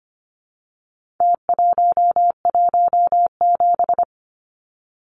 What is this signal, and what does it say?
Morse code, heard as a keyed tone, about 25 wpm, 700 Hz. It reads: T117
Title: Numbers - Contest Style - 4-Digit 25wpm